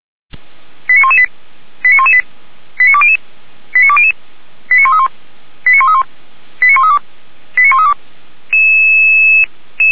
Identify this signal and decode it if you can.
Five-tone selective call (ZVEI), heard as quick runs of five tones, tones of about 70 ms. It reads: 89118 89118 89200 89200 89131 89131 89132 89132